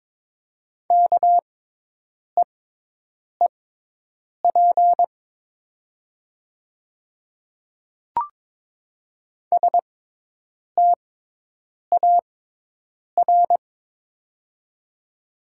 Morse code, heard as a keyed tone, about 22 wpm, spaced as for 8 wpm, 700 Hz. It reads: KEEP STAR
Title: 6x Character Spacing - Top 500 Words - 22wpm